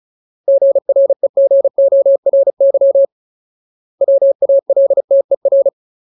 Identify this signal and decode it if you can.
Morse code, heard as a keyed tone, about 37 wpm, 550 Hz. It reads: GREGORY WALTER
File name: Ultimate Word List / 35WPM Ultimate